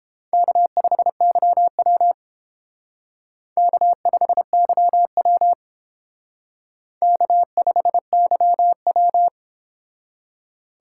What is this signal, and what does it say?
Morse code, heard as a keyed tone, about 30 wpm, 700 Hz. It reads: K5YW K5YW K5YW